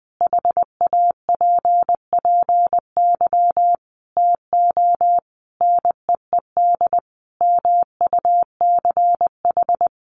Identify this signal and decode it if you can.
Morse code, heard as a keyed tone, about 20 wpm, 700 Hz. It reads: HAPPY TO NEED MUCH
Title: Sentences from Top 1000 Words 20wpm